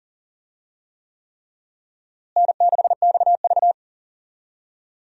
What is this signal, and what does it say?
Morse code, heard as a keyed tone, about 40 wpm, 700 Hz. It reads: N6XV
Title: USA Call Signs - 4 Characters - Repeated 3x - 40wpm